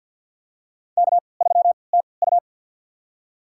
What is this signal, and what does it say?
Morse code, heard as a keyed tone, about 52 wpm, spaced as for 28 wpm, 700 Hz. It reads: K3TU
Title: USA Call Signs - 4 Characters - 3x Character Spacing - 50wpm